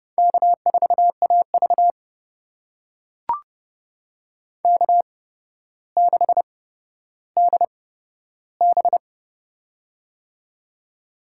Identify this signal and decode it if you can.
Morse code, heard as a keyed tone, about 31 wpm, spaced as for 13 wpm, 700 Hz. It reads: K4AV K6DB
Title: USA Call Signs - 4 Characters - 8x Character Spacing - 30wpm